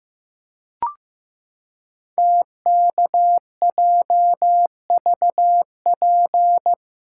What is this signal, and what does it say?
Morse code, heard as a keyed tone, about 15 wpm, 700 Hz. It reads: TKJVP
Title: Code Groups - 5 Letters - 15wpm